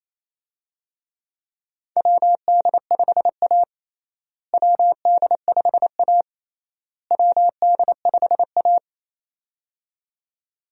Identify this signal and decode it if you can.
Morse code, heard as a keyed tone, about 27 wpm, 700 Hz. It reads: WD5A WD5A WD5A
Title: USA Call Signs - 4 Characters - Repeated 3x - 28wpm